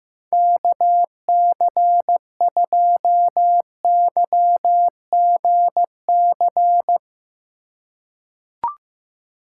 Morse code, heard as a keyed tone, about 15 wpm, 700 Hz. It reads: KC2YGC
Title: USA Call Signs - 6 Characters - 4x Character Spacing - 15wpm